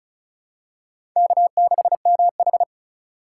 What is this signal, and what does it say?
Morse code, heard as a keyed tone, about 34 wpm, 700 Hz. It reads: K6MH
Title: USA Call Signs - 6x Character Spacing - 35wpm